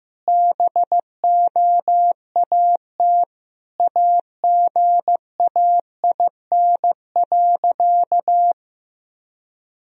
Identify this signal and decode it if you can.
Morse code, heard as a keyed tone, about 15 wpm, 700 Hz. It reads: BOAT AGAIN.
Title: Sentences from Top 1000 Words - Review Part 3 - 15wpm